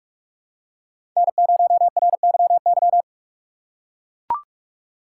Morse code, heard as a keyed tone, about 44 wpm, 700 Hz. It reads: N0RYY
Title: USA Call Signs - 5 Characters - 6x Character Spacing - 45wpm